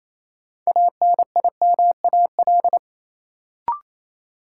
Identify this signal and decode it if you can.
Morse code, heard as a keyed tone, about 28 wpm, 700 Hz. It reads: ANIMAL
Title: Sentences from Top 1000 Words - Review Part 1 - New Word Drill - 28wpm